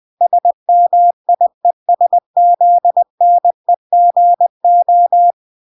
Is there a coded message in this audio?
Yes: Morse code from 0.2 to 5.3 s, heard as a keyed tone, about 20 wpm, 700 Hz. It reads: SMIESZNEGO